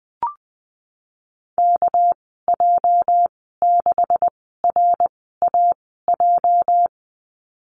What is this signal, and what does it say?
Morse code, heard as a keyed tone, about 20 wpm, spaced as for 15 wpm, 700 Hz. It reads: KJ6RAJ
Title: 207 Lesson 33g - Letter j - Call Signs 20wpm